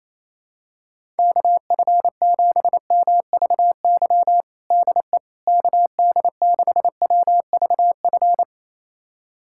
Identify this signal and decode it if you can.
Morse code, heard as a keyed tone, about 27 wpm, 700 Hz. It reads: KF7MVY DE KD6WVF
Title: USA Callsign DE Callsign - 6 Characters - 28wpm